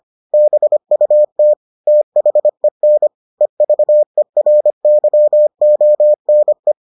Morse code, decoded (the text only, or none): BUT THEN EVERYONE